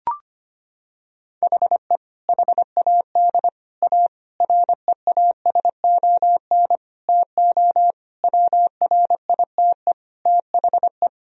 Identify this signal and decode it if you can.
Morse code, heard as a keyed tone, about 25 wpm, 700 Hz. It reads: HE HAD A REASON TO WRITE THE
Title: Sentences from Top 1000 Words - Review Part 1 - 25wpm